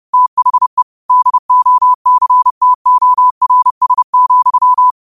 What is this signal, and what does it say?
Morse code, heard as a keyed tone, about 30 wpm, 1000 Hz. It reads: THE DOCTORS,
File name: qotdfiles / 30wpm